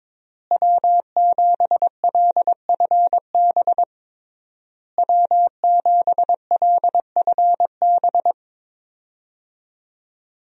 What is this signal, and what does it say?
Morse code, heard as a keyed tone, about 22 wpm, 700 Hz. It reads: W7LFB W7LFB
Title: USA Call Signs - 5 Characters - Repeated 3x - 22wpm